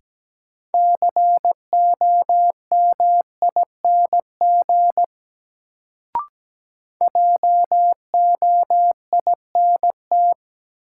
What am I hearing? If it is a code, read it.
Morse code, heard as a keyed tone, about 17 wpm, 700 Hz. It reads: COMING JOINT
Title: Sentences from Top 1300 Words - New Word Drill Alt - 17wpm